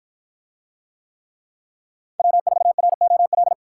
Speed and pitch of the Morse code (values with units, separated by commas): 52 wpm, 700 Hz